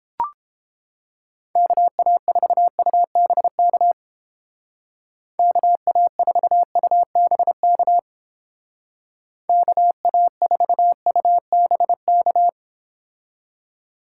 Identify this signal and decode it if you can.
Morse code, heard as a keyed tone, about 30 wpm, 700 Hz. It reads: KA4UBK KA4UBK KA4UBK